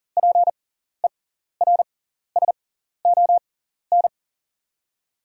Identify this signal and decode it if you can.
Morse code, heard as a keyed tone, about 40 wpm, spaced as for 14 wpm, 690 Hz. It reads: PERSON
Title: 6x Character Spacing - Top 500 Words - 40wpm